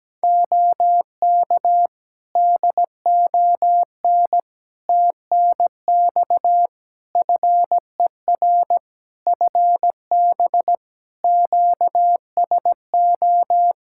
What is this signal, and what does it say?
Morse code, heard as a keyed tone, about 17 wpm, 700 Hz. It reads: OK DON TNX FER FB QSO
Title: Practice QSO - Exchange 4 - Ending - 17wpm